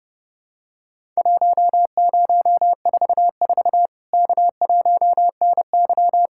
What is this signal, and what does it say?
Morse code, heard as a keyed tone, about 30 wpm, 700 Hz. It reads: 1044 K1NY
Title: Worked All Europe DX Contest - QTC Exchange - 30wpm